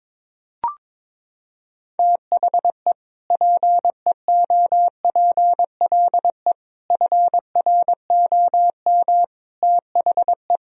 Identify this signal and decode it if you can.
Morse code, heard as a keyed tone, about 22 wpm, 700 Hz. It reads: THE PEOPLE FROM THE
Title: Sentences from Top 1000 Words - Review Part 3 - 22wpm